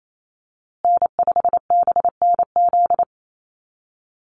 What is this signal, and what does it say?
Morse code, heard as a keyed tone, about 28 wpm, 700 Hz. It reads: N5BNZ
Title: USA Call Signs Repeated 3x 28wpm